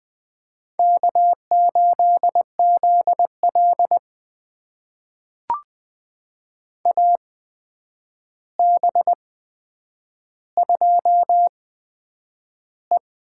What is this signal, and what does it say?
Morse code, heard as a keyed tone, about 20 wpm, spaced as for 9 wpm, 700 Hz. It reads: K8ZL AB2E
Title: USA Call Signs - 4 Characters - 8x Character Spacing - 20wpm